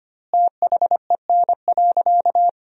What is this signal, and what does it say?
Morse code, heard as a keyed tone, about 25 wpm, 700 Hz. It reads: THEN.
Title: 2x Word-Spacing - Sentences from Top 300 Words - 25wpm